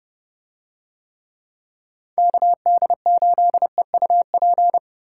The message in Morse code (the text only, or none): KD8EUP